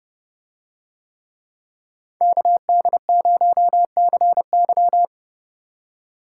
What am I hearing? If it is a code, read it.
Morse code, heard as a keyed tone, about 28 wpm, 700 Hz. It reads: KD0CY